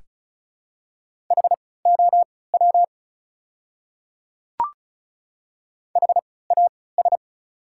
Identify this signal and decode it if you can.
Morse code, heard as a keyed tone, about 34 wpm, spaced as for 20 wpm, 690 Hz. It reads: HOW HAS
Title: 3x Character Spacing - Top 100 Words - 35wpm